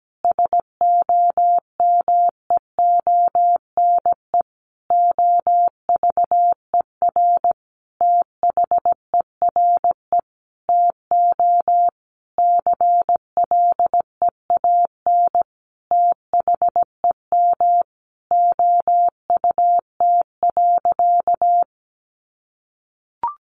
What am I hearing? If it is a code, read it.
Morse code, heard as a keyed tone, about 17 wpm, 700 Hz. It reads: SOMEONE OVER THERE TO CLEAN THEM OUT.
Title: Sentences from Top 1500 Words 17wpm